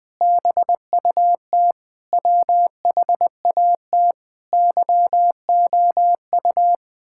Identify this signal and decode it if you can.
Morse code, heard as a keyed tone, about 20 wpm, 700 Hz. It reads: BUT WHAT YOU
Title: Sentences with 9-Word Phrases - 20wpm